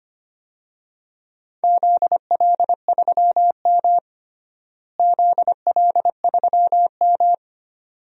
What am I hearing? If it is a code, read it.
Morse code, heard as a keyed tone, about 25 wpm, 700 Hz. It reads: ZL3M ZL3M
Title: DXCC-100 Call Signs - 4 Characters - Repeated 2x - 25wpm